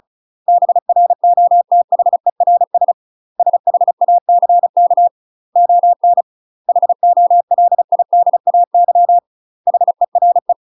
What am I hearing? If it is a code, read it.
Morse code, heard as a keyed tone, about 34 wpm, 700 Hz. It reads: BROTHERS SHACK ON HOLIDAY HERE
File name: UK MASTER QSOs TRAINING / 35WPM / UK MASTER QSOs 0008